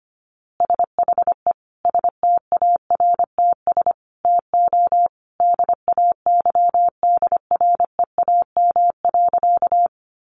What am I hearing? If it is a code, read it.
Morse code, heard as a keyed tone, about 25 wpm, 700 Hz. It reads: SHE STARTS TO DAYDREAM.
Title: Easy Phrases 25wpm